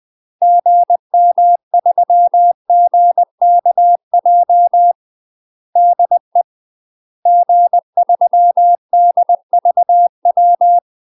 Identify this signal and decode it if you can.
Morse code, heard as a keyed tone, about 20 wpm, 700 Hz. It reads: GM3GKJ DE G3DVW
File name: UK MASTER QSOs TRAINING / 20WPM / UK MASTER QSOs 0031